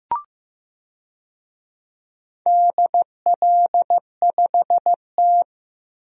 Morse code, heard as a keyed tone, about 16 wpm, 700 Hz. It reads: DL5T
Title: DXCC-100 Call Signs - 4 Characters - Repeated 3x - 15wpm